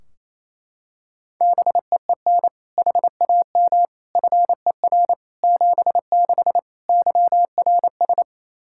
Morse code, heard as a keyed tone, about 28 wpm, 700 Hz. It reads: BEEN HAM FER 76 YRS
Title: Practice QSO - Exchange 3 - More Info - 28wpm